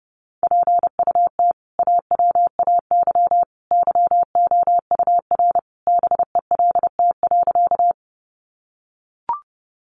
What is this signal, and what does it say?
Morse code, heard as a keyed tone, about 30 wpm, 700 Hz. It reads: PUT AWAY YOUR BELT.
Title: Sentences from Top 1400 Words 30wpm